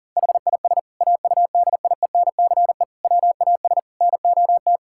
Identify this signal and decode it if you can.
Morse code, heard as a keyed tone, about 40 wpm, 700 Hz. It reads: HIS AUDIENCE WAS NOT
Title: Sentences from Top 1000 Words - Review Part 5 - 40wpm